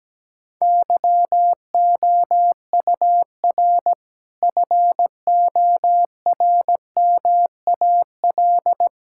Morse code, heard as a keyed tone, about 17 wpm, 700 Hz. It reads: YOUR FORMAL
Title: Sentences from Top 1600 Words 17wpm